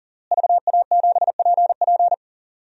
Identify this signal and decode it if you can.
Morse code, heard as a keyed tone, about 40 wpm, 700 Hz. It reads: VA7PP
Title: DXCC-100 Call Signs - 5 Characters - Repeated 3x - 40wpm